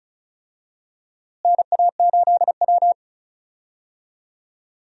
Morse code, heard as a keyed tone, about 36 wpm, 700 Hz. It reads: NA8W
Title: USA Call Signs - 4 Characters - Repeated 3x - 35wpm